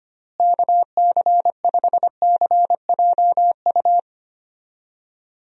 Morse code, heard as a keyed tone, about 25 wpm, 700 Hz. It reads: KC5CJU
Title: USA Call Signs - 6 Characters - Repeated 3x - 25wpm